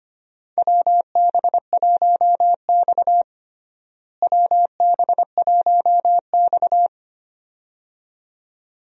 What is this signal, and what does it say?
Morse code, heard as a keyed tone, about 25 wpm, 700 Hz. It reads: WB1X WB1X
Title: USA Call Signs - 4 Characters - Repeated 2x - 25wpm